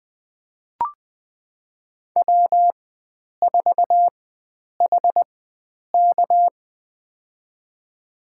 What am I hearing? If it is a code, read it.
Morse code, heard as a keyed tone, about 21 wpm, spaced as for 9 wpm, 700 Hz. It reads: W4HK
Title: USA Call Signs - 4 Characters - 4x Character Spacing - 20wpm